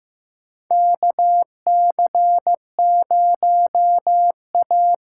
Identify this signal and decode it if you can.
Morse code, heard as a keyed tone, about 15 wpm, 700 Hz. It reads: KC0A
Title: USA Call Signs - 6 Characters - 8x Character Spacing - 15wpm